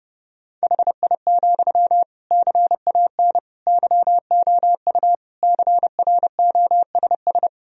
Morse code, heard as a keyed tone, about 30 wpm, 700 Hz. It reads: HI, CAN YOU CROSS
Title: Sentences from Top 1000 Words - Review Part 3 - 30wpm